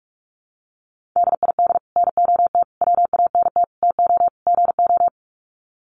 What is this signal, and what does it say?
Morse code, heard as a keyed tone, about 44 wpm, 700 Hz. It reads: DID NOT WANT TO GO
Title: Sets of 5 Words 45wpm